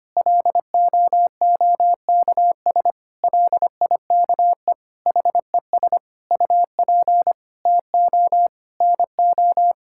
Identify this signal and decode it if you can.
Morse code, heard as a keyed tone, about 25 wpm, 700 Hz. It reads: LOOKS LIKE HES UP TO NO
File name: Sets of 9 Words - 25wpm